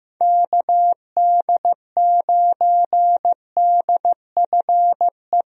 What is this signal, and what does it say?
Morse code, heard as a keyed tone, about 15 wpm, 700 Hz. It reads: KD9DFE
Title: USA Call Signs - 6 Characters - Repeated 2x - 15wpm